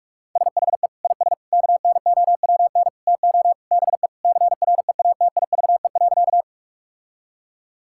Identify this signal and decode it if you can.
Morse code, heard as a keyed tone, about 41 wpm, 700 Hz. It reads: SHE IS KNOWN TO BE CREATIVE.
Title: Sentences from Top 1000 Words - Review Part 5 - 45wpm